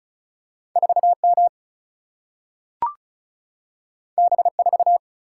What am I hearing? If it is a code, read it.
Morse code, heard as a keyed tone, about 36 wpm, 700 Hz. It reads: 4M B4
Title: Single-Letter Single-Number 35wpm